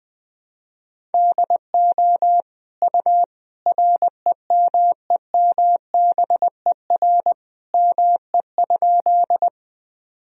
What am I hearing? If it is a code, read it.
Morse code, heard as a keyed tone, about 20 wpm, 700 Hz. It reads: DO U REMEMBER ME?
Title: Sentences from Top 400 Words and Abbreviations 20wpm